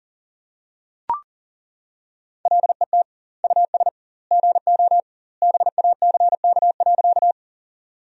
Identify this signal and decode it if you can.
Morse code, heard as a keyed tone, about 40 wpm, 700 Hz. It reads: LET US GO BACK.
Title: 2x Word-Spacing - Sentences from Top 300 Words - 40wpm